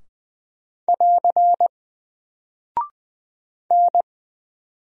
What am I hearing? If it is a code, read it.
Morse code, heard as a keyed tone, about 20 wpm, 700 Hz. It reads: <AR> N
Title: Single Character (Letter:Number:Punctuation:Prosign) Alt 20wpm